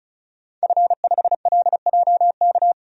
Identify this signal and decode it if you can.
Morse code, heard as a keyed tone, about 34 wpm, 700 Hz. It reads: F5LJK
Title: DXCC-100 Call Signs - 5 Characters - 35wpm